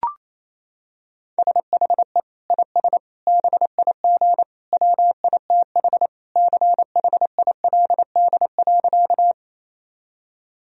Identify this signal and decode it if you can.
Morse code, heard as a keyed tone, about 28 wpm, 700 Hz. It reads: SHE IS BIG WITH CHILD.
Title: Sentences from Top 1000 Words - Review Part 1 - 28wpm